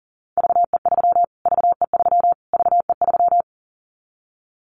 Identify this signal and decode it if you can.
Morse code, heard as a keyed tone, about 40 wpm, 700 Hz. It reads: VE3 VE3 VE3
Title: Callsign Prefixes for Top 25 DXCC - Repeated 3x - 40wpm